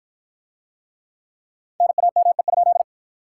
Morse code, heard as a keyed tone, about 54 wpm, 700 Hz. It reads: NAME?